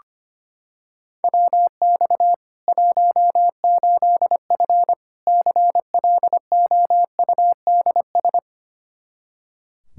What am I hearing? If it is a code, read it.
Morse code, heard as a keyed tone, about 25 wpm, 700 Hz. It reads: WX 18F CLOUDS
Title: Practice QSO - Exchange 2 - Info - 25wpm